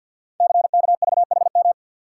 Morse code, heard as a keyed tone, about 48 wpm, 690 Hz. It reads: CKVHM
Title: Code Groups - 5 Letters - 50wpm